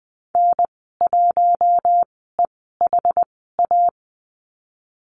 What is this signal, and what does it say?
Morse code, heard as a keyed tone, about 20 wpm, spaced as for 14 wpm, 700 Hz. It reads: N1EHA
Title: 069 Lesson 12g - Letter d - Call Signs 20wpm